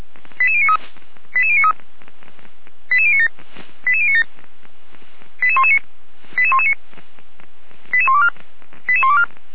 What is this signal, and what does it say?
Five-tone selective call (ZVEI), heard as quick runs of five tones, tones of about 70 ms. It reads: 88093 88093 88097 88097 88119 88119 88124 88124
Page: /455/SDR/ZVEI/